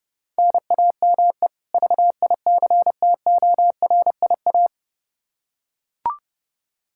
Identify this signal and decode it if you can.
Morse code, heard as a keyed tone, about 30 wpm, 700 Hz. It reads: NAME VICTORIA
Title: Practice QSO - Exchange 1 - Beginning - 30wpm